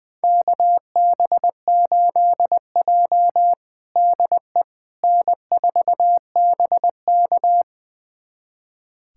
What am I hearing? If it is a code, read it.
Morse code, heard as a keyed tone, about 21 wpm, 700 Hz. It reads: KB8J DE N4BK
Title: USA Callsign DE Callsign - 4 Characters - 20wpm